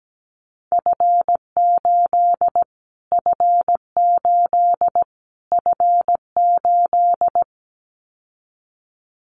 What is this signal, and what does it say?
Morse code, heard as a keyed tone, about 17 wpm, 700 Hz. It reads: F8 F8 F8
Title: Callsign Prefixes for Top 25 DXCC - Repeated 3x - 17wpm